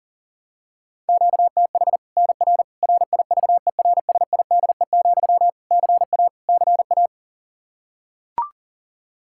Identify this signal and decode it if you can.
Morse code, heard as a keyed tone, about 37 wpm, 700 Hz. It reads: QTH NR RIVERSIDE, CA CA
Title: Practice QSO - Exchange 1 - Beginning - 40wpm